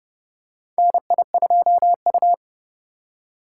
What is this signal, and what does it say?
Morse code, heard as a keyed tone, about 30 wpm, 700 Hz. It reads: NI2U
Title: USA Call Signs - 2x Character Spacing - 30wpm